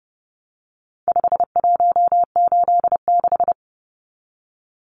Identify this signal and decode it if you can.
Morse code, heard as a keyed tone, about 30 wpm, 700 Hz. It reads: 5186
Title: Numbers - 4-Digit 30wpm